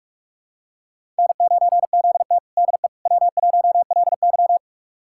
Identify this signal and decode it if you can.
Morse code, heard as a keyed tone, about 46 wpm, 700 Hz. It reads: N9ZT DE W1LY
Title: USA Callsign DE Callsign - 4 Characters - 45wpm